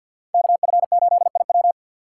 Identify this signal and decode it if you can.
Morse code, heard as a keyed tone, about 63 wpm, 700 Hz. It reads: KF8IW